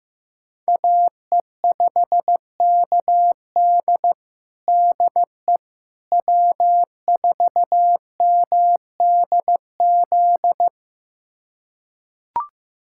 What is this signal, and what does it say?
Morse code, heard as a keyed tone, about 15 wpm, 700 Hz. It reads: AE5KD DE W4MDZ
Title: USA Callsign DE Callsign - 5 Characters - 15wpm